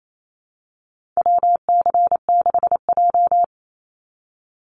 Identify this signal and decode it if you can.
Morse code, heard as a keyed tone, about 28 wpm, 700 Hz. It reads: WC6J
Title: NA SOTA Activator Callsigns 28wpm